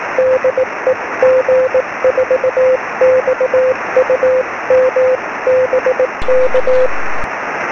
Morse code, heard as a keyed tone, about 18 wpm, 520 Hz, with noise. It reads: DE G4XUM BK
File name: sm7lcb / g4xum 23cm